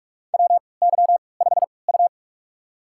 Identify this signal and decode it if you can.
Morse code, heard as a keyed tone, about 47 wpm, spaced as for 26 wpm, 700 Hz. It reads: WY5U
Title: USA Call Signs - 3x Character Spacing - 45wpm